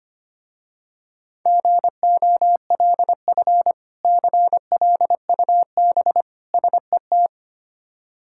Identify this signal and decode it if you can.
Morse code, heard as a keyed tone, about 25 wpm, 700 Hz. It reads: GOLF CLUB SET
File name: practice / 25wpm / Difficult Words with Hints - 25wpm